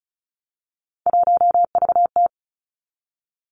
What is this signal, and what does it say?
Morse code, heard as a keyed tone, about 36 wpm, 700 Hz. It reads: 1VT
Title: Three Challenging Letter-Number Combinations 35wpm